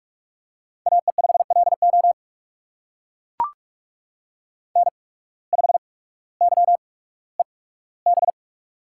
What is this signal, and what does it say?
Morse code, heard as a keyed tone, about 44 wpm, spaced as for 19 wpm, 690 Hz. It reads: AE5LO N5YEB
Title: USA Call Signs - 5 Characters - 8x Character Spacing - 45wpm